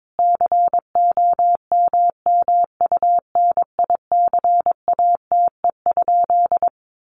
Morse code, heard as a keyed tone, about 22 wpm, 700 Hz. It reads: COMMUNICATE?
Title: Sentences from Top 1300 Words 22wpm